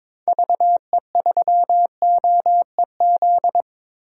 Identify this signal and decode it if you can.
Morse code, heard as a keyed tone, about 22 wpm, 700 Hz. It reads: VE3OEZ
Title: DXCC-100 Call Signs - 6 Characters - 22wpm